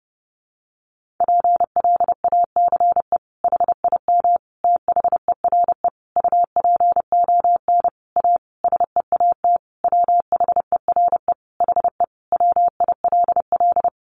Morse code, heard as a keyed tone, about 30 wpm, 700 Hz. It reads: PLACE HIM THERE UPON A SEAT WHERE HE WILL
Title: Sentences from Top 1500 Words 30wpm